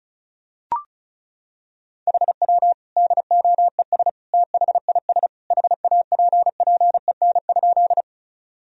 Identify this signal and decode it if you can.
Morse code, heard as a keyed tone, about 36 wpm, 700 Hz. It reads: HW DOES THIS HAPPEN?
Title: Sentences from Top 400 Words and Abbreviations 35wpm